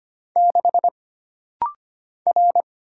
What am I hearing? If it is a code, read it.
Morse code, heard as a keyed tone, about 25 wpm, 700 Hz. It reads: 6 R
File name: Single Character (Letter:Number:Punctuation:Prosign) - Rapid-Fire - 25wpm